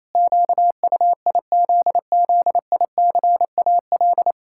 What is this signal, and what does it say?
Morse code, heard as a keyed tone, about 28 wpm, 700 Hz. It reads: QUIZZICAL
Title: Pangram Challenge v2 - 28wpm